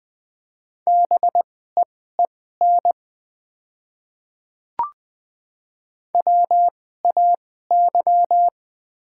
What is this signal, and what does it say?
Morse code, heard as a keyed tone, about 20 wpm, spaced as for 14 wpm, 700 Hz. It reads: BEEN WAY
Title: 2x Character Spacing - Top 200 Words - 20wpm